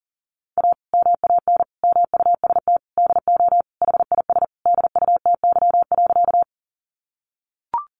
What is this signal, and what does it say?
Morse code, heard as a keyed tone, about 40 wpm, 700 Hz. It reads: A MAN MUST DO HIS DUTY.
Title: Sentences from Top 1500 Words 40wpm